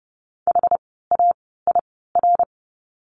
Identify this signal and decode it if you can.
Morse code, heard as a keyed tone, about 30 wpm, spaced as for 17 wpm, 700 Hz. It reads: HAIR
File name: Lesson 17f - Letter u - Words 30wpm